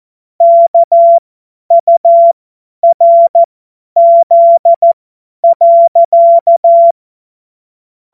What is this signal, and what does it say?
Morse code, heard as a keyed tone, about 14 wpm, spaced as for 10 wpm, 680 Hz. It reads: KURZ.
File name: Morsekurs / Witze / Witz 12 März 2024 50 F70